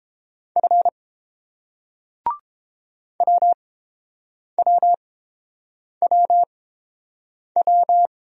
Morse code, heard as a keyed tone, about 30 wpm, spaced as for 7 wpm, 700 Hz. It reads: F WWWW